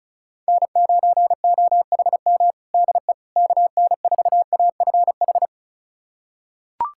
Morse code, heard as a keyed tone, about 34 wpm, 700 Hz. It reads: N9OHM DE KN4AFH
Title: USA Callsign DE Callsign 35wpm